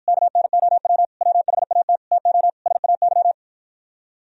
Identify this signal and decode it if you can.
Morse code, heard as a keyed tone, about 52 wpm, 700 Hz. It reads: KNOW WHAT TO SAY